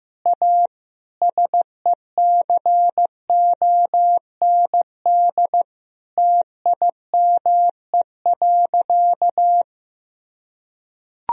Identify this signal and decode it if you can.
Morse code, heard as a keyed tone, about 15 wpm, 700 Hz. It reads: A SECOND TIME.
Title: Sentences from Top 1000 Words - Review Part 1 - 15wpm